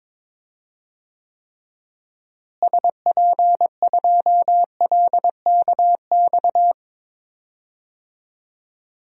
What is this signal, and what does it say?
Morse code, heard as a keyed tone, about 22 wpm, 700 Hz. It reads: SP2LKX
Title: DXCC-100 Call Signs - 6 Characters - Repeated 3x - 22wpm